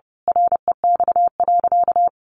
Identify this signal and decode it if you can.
Morse code, heard as a keyed tone, about 30 wpm, 700 Hz. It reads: REX.
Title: A Sound Of Thunder 30wpm